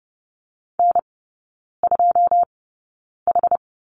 Morse code, heard as a keyed tone, about 30 wpm, spaced as for 9 wpm, 700 Hz. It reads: N2H
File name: 081 Lesson 14e - Number 2 - Three Characters 30wpm